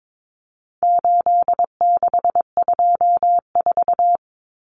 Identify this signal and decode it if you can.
Morse code, heard as a keyed tone, about 22 wpm, 700 Hz. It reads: 8624
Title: Common Cut Numbers - 4-Digit 22wpm